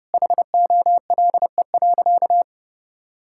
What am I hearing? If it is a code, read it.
Morse code, heard as a keyed tone, about 32 wpm, 700 Hz. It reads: HOLE.
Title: Sentences from Top 1600 Words 30wpm